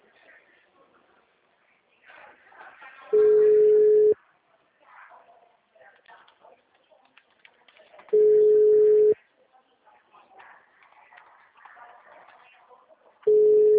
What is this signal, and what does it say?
Ringback tone (Europe, 425 Hz), with speech.